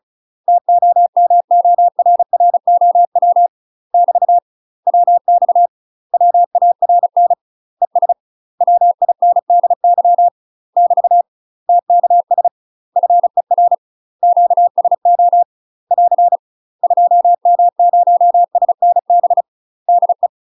Morse code, heard as a keyed tone, about 36 wpm, 700 Hz. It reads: TOMORROW <BT> WX WARN ES WINDY <BT> TKS FER QSO <AR> 2M0SNB DE
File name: UK MASTER QSOs TRAINING / 35WPM / UK MASTER QSOs 0010